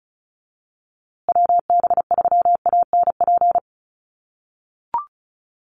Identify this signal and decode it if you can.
Morse code, heard as a keyed tone, about 34 wpm, 700 Hz. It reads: WB3ANP
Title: USA Call Signs Repeated 2x 35wpm